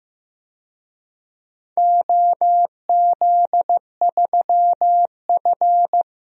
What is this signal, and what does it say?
Morse code, heard as a keyed tone, about 15 wpm, 700 Hz. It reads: OZ3F
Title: DXCC-100 Call Signs - 4 Characters - 15wpm